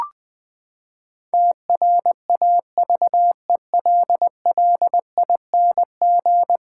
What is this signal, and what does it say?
Morse code, heard as a keyed tone, about 19 wpm, 700 Hz. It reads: TRAVELLING
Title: Pangram Challenge v2 - 20wpm